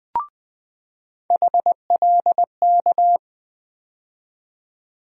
Morse code, heard as a keyed tone, about 20 wpm, 700 Hz. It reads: HLK
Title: USA Call Sign Postfixes - 3 Letters 20wpm